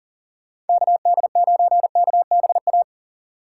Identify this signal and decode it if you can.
Morse code, heard as a keyed tone, about 40 wpm, 700 Hz. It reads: KD9KBA
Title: USA Call Signs - 6 Characters - 4x Character Spacing - 40wpm